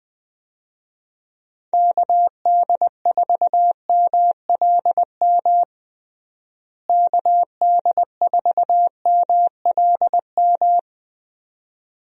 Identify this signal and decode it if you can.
Morse code, heard as a keyed tone, about 20 wpm, 700 Hz. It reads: KD4MLM KD4MLM
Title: USA Call Signs - 6 Characters - Repeated 2x - 20wpm